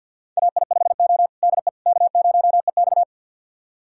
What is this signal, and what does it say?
Morse code, heard as a keyed tone, about 50 wpm, 700 Hz. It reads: AI5O DE K0EX
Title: USA Callsign DE Callsign - 4 Characters - 50wpm